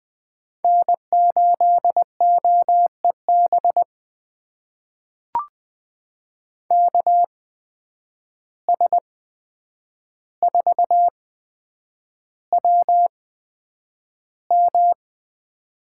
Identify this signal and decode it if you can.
Morse code, heard as a keyed tone, about 20 wpm, spaced as for 9 wpm, 700 Hz. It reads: N8OEB KS4WM
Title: USA Call Signs - 8x Character Spacing - 20wpm